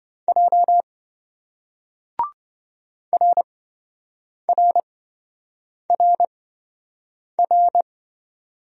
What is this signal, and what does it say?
Morse code, heard as a keyed tone, about 26 wpm, spaced as for 7 wpm, 700 Hz. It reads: J RRRR